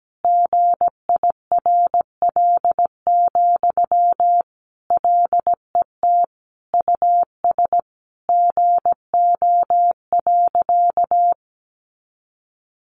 Morse code, heard as a keyed tone, about 17 wpm, 700 Hz. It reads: GIRL, LET US GO.
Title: Sentences from Top 400 Words 17wpm